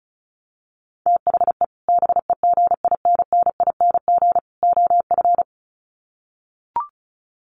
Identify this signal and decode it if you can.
Morse code, heard as a keyed tone, about 34 wpm, 700 Hz. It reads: THE BEGINNING OF
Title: Sets of 3 Words 35wpm